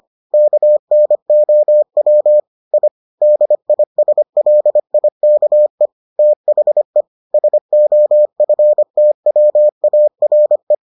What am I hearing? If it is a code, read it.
Morse code, heard as a keyed tone, about 25 wpm, 600 Hz. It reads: KNOW I DISLIKE THE SOFTWARE